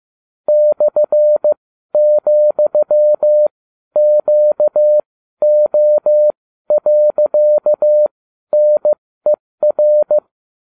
Morse code, heard as a keyed tone, about 15 wpm, spaced as for 12 wpm, 600 Hz. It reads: /,QO.NER